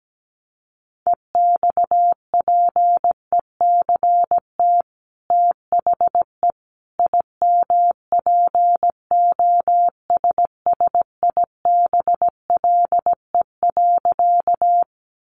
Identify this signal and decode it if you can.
Morse code, heard as a keyed tone, about 17 wpm, 700 Hz. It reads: EXPECT THE IMPOSSIBLE.
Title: Sentences from Top 1100 Words 17wpm